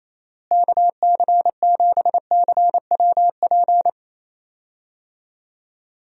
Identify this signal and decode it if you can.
Morse code, heard as a keyed tone, about 28 wpm, 700 Hz. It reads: KC7CWP
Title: USA Call Signs - 6 Characters - Repeated 3x - 28wpm